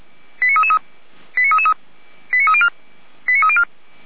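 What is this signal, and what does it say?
Five-tone selective call (ZVEI), heard as quick runs of five tones, tones of about 70 ms. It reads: 89333 89333 89304 89304
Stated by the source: ZVEI five-tone selective call received by SDR, code 89230, archived by Feuerwehr Orlamünde.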